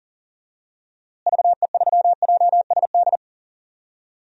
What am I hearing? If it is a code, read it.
Morse code, heard as a keyed tone, about 40 wpm, 700 Hz. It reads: VE3JSD
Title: DXCC-100 Call Signs - 6 Characters - Repeated 3x - 40wpm